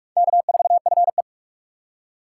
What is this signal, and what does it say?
Morse code, heard as a keyed tone, about 41 wpm, 700 Hz. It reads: K4FE